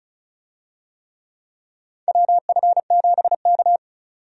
Morse code, heard as a keyed tone, about 34 wpm, 700 Hz. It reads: WF7K
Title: USA Call Signs - 4 Characters - 35wpm